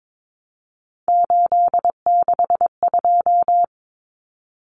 Morse code, heard as a keyed tone, about 22 wpm, 700 Hz. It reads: 862
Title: Common Cut Numbers - 3-Digit 22wpm